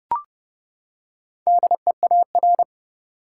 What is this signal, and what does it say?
Morse code, heard as a keyed tone, about 30 wpm, 700 Hz. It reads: DEAR
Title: Four Letter Words - 30wpm